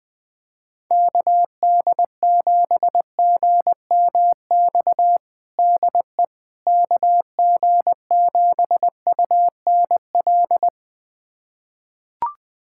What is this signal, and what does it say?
Morse code, heard as a keyed tone, about 21 wpm, 700 Hz. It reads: KD7GMX DE KG7UNL
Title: USA Callsign DE Callsign - 6 Characters - 20wpm